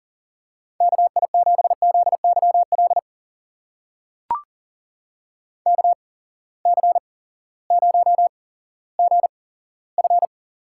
Morse code, heard as a keyed tone, about 40 wpm, spaced as for 18 wpm, 700 Hz. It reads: KI7ZYL KC0GF
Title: USA Call Signs - 6 Characters - 8x Character Spacing - 40wpm